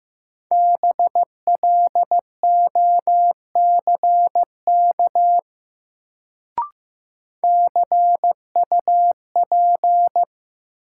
Morse code, heard as a keyed tone, about 15 wpm, 700 Hz. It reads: BLOCK CUP
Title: Sentences from Top 1600 Words - New Word Drill - 15wpm